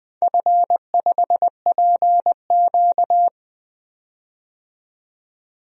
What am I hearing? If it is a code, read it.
Morse code, heard as a keyed tone, about 20 wpm, 700 Hz. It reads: F5PQ
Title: DXCC-100 Call Signs - 4 Characters - 20wpm